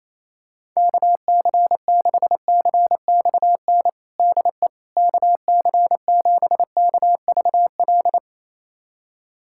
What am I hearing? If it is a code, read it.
Morse code, heard as a keyed tone, about 28 wpm, 700 Hz. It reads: KC6CXN DE KC7KVL
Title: USA Callsign DE Callsign - 6 Characters - 28wpm